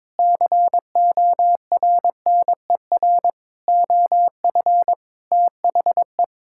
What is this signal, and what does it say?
Morse code, heard as a keyed tone, about 22 wpm, 700 Hz. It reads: CORNER OF THE
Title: Sentences from Top 1600 Words 22wpm